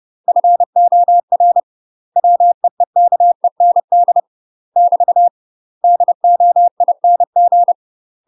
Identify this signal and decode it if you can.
Morse code, heard as a keyed tone, about 30 wpm, 700 Hz. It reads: FOR WEEKEND <BT> DOING
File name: UK MASTER QSOs TRAINING / 30WPM / UK MASTER QSOs 0034